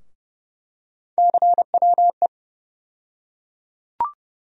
Morse code, heard as a keyed tone, about 31 wpm, 700 Hz. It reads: CWE
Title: USA Call Sign Postfixes - 3 Letters 30wpm